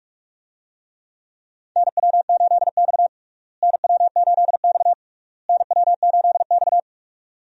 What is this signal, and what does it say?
Morse code, heard as a keyed tone, about 46 wpm, 700 Hz. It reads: NW8X NW8X NW8X
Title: USA Call Signs - 4 Characters - Repeated 3x - 45wpm